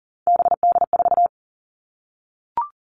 Morse code, heard as a keyed tone, about 38 wpm, 700 Hz. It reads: BD4
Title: Callsign Prefixes for Top 100 DXCC - 40wpm